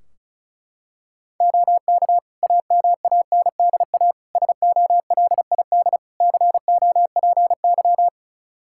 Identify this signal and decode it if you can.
Morse code, heard as a keyed tone, about 36 wpm, 700 Hz. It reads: OK AMANDA SOLID COPY
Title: K1USN SST QSO Practice 35wpm